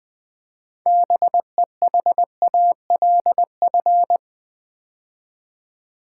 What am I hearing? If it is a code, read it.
Morse code, heard as a keyed tone, about 20 wpm, 700 Hz. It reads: BEHALF
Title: Build A Word - 3 Letters At A Time - 6-Letter Words - 20wpm